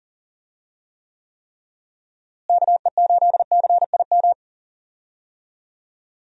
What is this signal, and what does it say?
Morse code, heard as a keyed tone, about 37 wpm, 700 Hz. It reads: KE8CIM
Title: USA Call Signs - 6 Characters - Repeated 3x - 40wpm